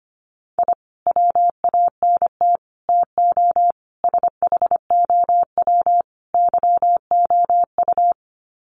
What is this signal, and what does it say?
Morse code, heard as a keyed tone, about 25 wpm, 700 Hz. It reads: I WANT TO SHOW YOU
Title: Sets of 5 Words 25wpm